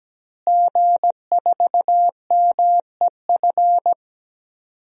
Morse code, heard as a keyed tone, about 17 wpm, 700 Hz. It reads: G4MEF
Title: DXCC-100 Call Signs - 5 Characters - Repeated 2x - 17wpm